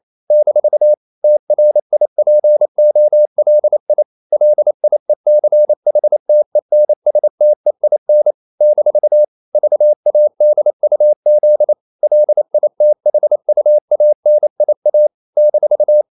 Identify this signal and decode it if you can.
Morse code, heard as a keyed tone, about 28 wpm, 600 Hz. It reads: - TRIPOLI LIECHTENSTEIN - VADUZ LITHUANIA -
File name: CW / Practice / NationalCapitals28